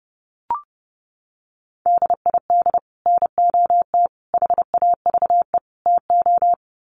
Morse code, heard as a keyed tone, about 30 wpm, 700 Hz. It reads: DID NOT HAVE TO
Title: Sets of 4 Words 30wpm